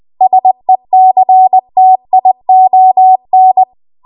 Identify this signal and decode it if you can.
Morse code, heard as a keyed tone, about 20 wpm, 750 Hz. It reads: SECTION
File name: Archive / 20 WPM / 20WPM